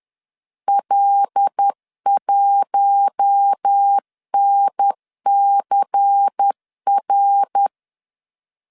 Morse code, heard as a keyed tone, about 10 wpm, 800 Hz. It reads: L1NCR
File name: LEZIONI / 6 PKX0 / 6 PKX0 10wpm 2707